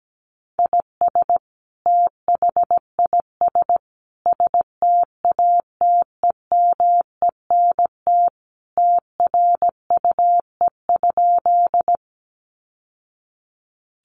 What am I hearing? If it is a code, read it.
Morse code, heard as a keyed tone, about 17 wpm, 700 Hz. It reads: IS THIS STATEMENT TRUE?
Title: Sentences from Top 1200 Words 17wpm